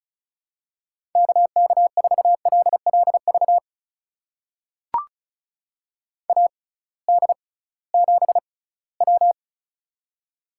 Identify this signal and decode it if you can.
Morse code, heard as a keyed tone, about 36 wpm, spaced as for 21 wpm, 700 Hz. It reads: KK4LLV AD7W
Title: USA Call Signs - 6x Character Spacing - 35wpm